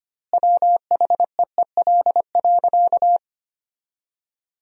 Morse code, heard as a keyed tone, about 25 wpm, 700 Hz. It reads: WHEEL.
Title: 2x Word-Spacing - Sentences from Top 500 Words - 25wpm